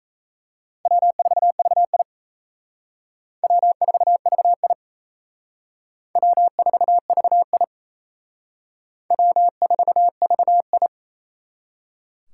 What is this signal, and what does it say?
Morse code, heard as a keyed tone, about 34 wpm, 700 Hz. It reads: W4VI W4VI W4VI W4VI
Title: Speed Racing - USA 1x2 Call Signs 28wpm